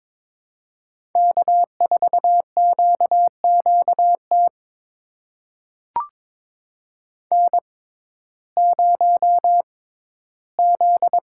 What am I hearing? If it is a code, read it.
Morse code, heard as a keyed tone, about 22 wpm, spaced as for 14 wpm, 700 Hz. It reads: K4QQT N0Z
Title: USA Call Signs - 5 Characters - 6x Character Spacing - 22wpm